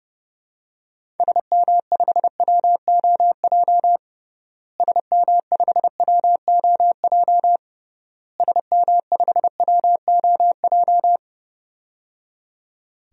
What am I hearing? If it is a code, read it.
Morse code, heard as a keyed tone, about 30 wpm, 700 Hz. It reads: SM5WOJ SM5WOJ SM5WOJ
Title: DXCC-100 Call Signs - 6 Characters - Repeated 3x - 30wpm